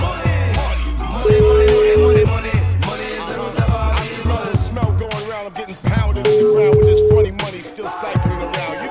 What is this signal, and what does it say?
Ringback tone (Europe, 425 Hz), with noise.